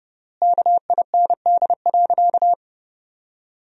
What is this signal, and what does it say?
Morse code, heard as a keyed tone, about 30 wpm, 700 Hz. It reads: KIND.
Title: 3x Word-Spacing - Sentences from Top 500 Words - 30wpm